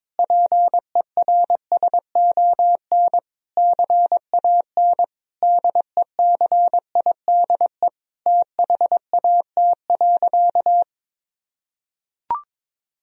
Morse code, heard as a keyed tone, about 21 wpm, 700 Hz. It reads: PERSON CAN DECIDE THAT.
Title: Sentences from Top 1000 Words - Review Part 3 - 22wpm